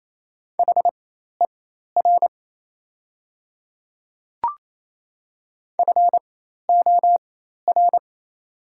Morse code, heard as a keyed tone, about 27 wpm, spaced as for 13 wpm, 690 Hz. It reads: HER FOR
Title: 4x Character Spacing - Top 100 Words - 28wpm